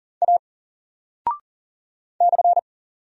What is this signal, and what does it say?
Morse code, heard as a keyed tone, about 38 wpm, 700 Hz. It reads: A /
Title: Single Character (Letter:Number:Punctuation:Prosign) Alt 40wpm